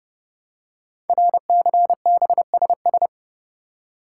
Morse code, heard as a keyed tone, about 30 wpm, 700 Hz. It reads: RCBSS
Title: Code Groups - 5 Letters - 30wpm